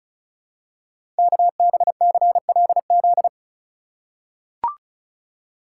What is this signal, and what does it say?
Morse code, heard as a keyed tone, about 36 wpm, 700 Hz. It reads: KBCLZ
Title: Code Groups - 5 Letters - 35wpm